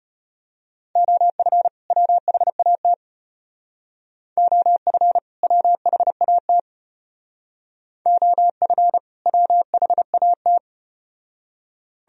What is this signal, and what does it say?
Morse code, heard as a keyed tone, about 35 wpm, 700 Hz. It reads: OF WHAT OF WHAT OF WHAT